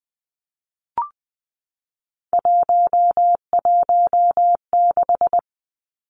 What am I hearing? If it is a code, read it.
Morse code, heard as a keyed tone, about 20 wpm, 700 Hz. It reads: 116
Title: Numbers - 3-Digit 20wpm